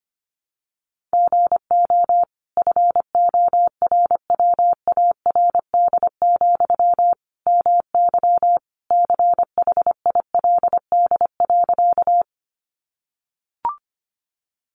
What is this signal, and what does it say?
Morse code, heard as a keyed tone, about 25 wpm, 700 Hz. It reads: GO FORWARD, MY CHILD.
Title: Sentences from Top 1300 Words 25wpm